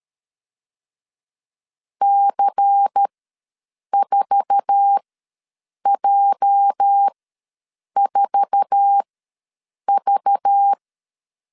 Morse code, heard as a keyed tone, about 13 wpm, spaced as for 7 wpm, 800 Hz. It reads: C4J4V
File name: LEZIONI / 8 YJ45 / 8 YJ4 ws 12wpm 3249